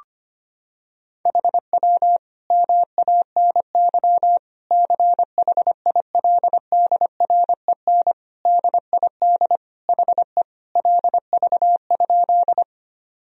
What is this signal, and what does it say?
Morse code, heard as a keyed tone, about 25 wpm, 700 Hz. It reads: HW MANY CHILDREN DID HE LV?
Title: Sentences from Top 400 Words and Abbreviations 25wpm